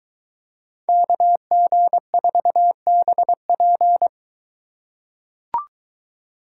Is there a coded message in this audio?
Yes: Morse code from 0.9 to 4.1 s, heard as a keyed tone, about 23 wpm, 700 Hz. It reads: KG4BP